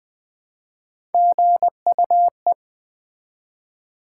Morse code, heard as a keyed tone, about 20 wpm, 700 Hz. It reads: GUE
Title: Build A Word - 1 Letter At A Time - 3 to 5-Letter Words - 20wpm